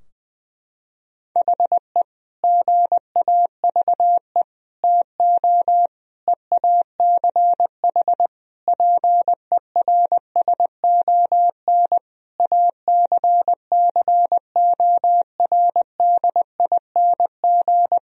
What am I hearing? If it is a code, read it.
Morse code, heard as a keyed tone, about 21 wpm, 700 Hz. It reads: HE GAVE TO EACH PERSON ACCORDING
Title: Sentences from Top 1000 Words - Review Part 4 - 20wpm